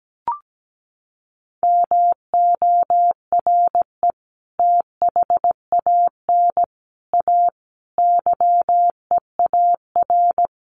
Morse code, heard as a keyed tone, about 17 wpm, 700 Hz. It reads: MORE THAN A YEAR
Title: Sets of 5 Words Encore2 17wpm